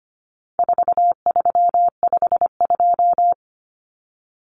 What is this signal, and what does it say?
Morse code, heard as a keyed tone, about 25 wpm, 700 Hz. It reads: 4352
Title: Common Cut Numbers - 4-Digit 25wpm